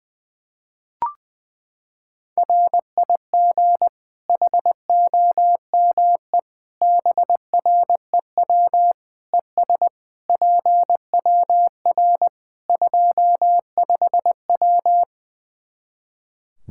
Morse code, heard as a keyed tone, about 20 wpm, 700 Hz. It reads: RIG HOME BREW ES PWR 25W
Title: Practice QSO - Exchange 2 - Info - 20wpm